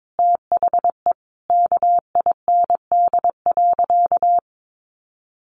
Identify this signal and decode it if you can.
Morse code, heard as a keyed tone, about 22 wpm, 700 Hz. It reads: THE KIND.
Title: Sentences from Top 1200 Words 22wpm